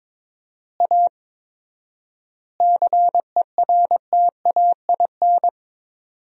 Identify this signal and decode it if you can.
Morse code, heard as a keyed tone, about 22 wpm, 700 Hz. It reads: A CERTAIN
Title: 4x Word-Spacing - Sentences from Top 500 Words - 22wpm